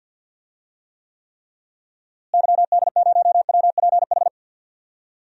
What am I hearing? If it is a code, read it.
Morse code, heard as a keyed tone, about 51 wpm, 700 Hz. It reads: YD0WPH
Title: DXCC-100 Call Signs - 6 Characters - Repeated 3x - 50wpm